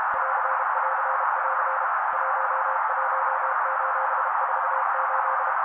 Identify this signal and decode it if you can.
Morse code, heard as a keyed tone, about 32 wpm, 530 Hz. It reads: <AR><AR><AR> 11:48
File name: daily / 31wpm noise